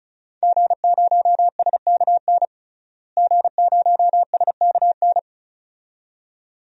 Morse code, heard as a keyed tone, about 34 wpm, 700 Hz. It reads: G0SKN G0SKN
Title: DXCC-100 Call Signs - 5 Characters - Repeated 2x - 35wpm